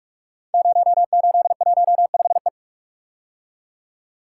Morse code, heard as a keyed tone, about 44 wpm, 700 Hz. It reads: 0815E
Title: Code Groups - 5 Letter:Numbers - 45wpm